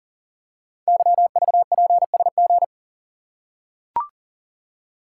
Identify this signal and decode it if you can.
Morse code, heard as a keyed tone, about 40 wpm, 700 Hz. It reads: YVPSG
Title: Code Groups - 5 Letters - 40wpm